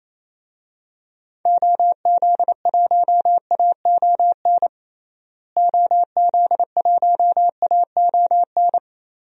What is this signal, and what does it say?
Morse code, heard as a keyed tone, about 28 wpm, 700 Hz. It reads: OZ1AON OZ1AON
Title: DXCC-100 Call Signs - 6 Characters - Repeated 3x - 28wpm